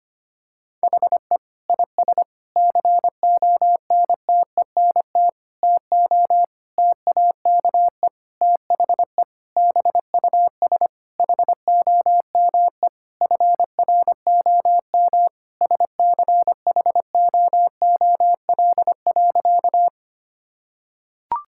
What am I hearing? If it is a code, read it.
Morse code, heard as a keyed tone, about 25 wpm, 700 Hz. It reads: HE IS CONTENT TO TAKE THE BUS HOME FROM SCHOOL.
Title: Sentences from Top 1000 Words - Review Part 6 - 25wpm